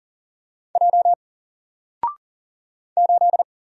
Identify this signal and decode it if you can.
Morse code, heard as a keyed tone, about 40 wpm, 700 Hz. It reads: J 8
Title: Single Letter-Number 40wpm